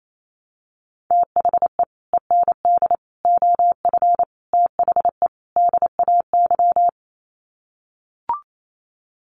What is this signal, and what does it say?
Morse code, heard as a keyed tone, about 28 wpm, 700 Hz. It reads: THE END OF THE DAY
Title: Sets of 5 Words 28wpm